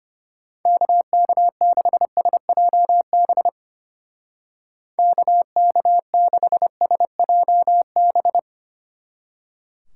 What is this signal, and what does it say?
Morse code, heard as a keyed tone, about 31 wpm, 700 Hz. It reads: KK6SJB KK6SJB
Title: Speed Racing - USA 2x3 Call Signs 25wpm